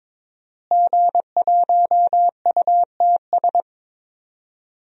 Morse code, heard as a keyed tone, about 22 wpm, 700 Hz. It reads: G1UTS
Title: DXCC-100 Call Signs - 5 Characters - Repeated 2x - 22wpm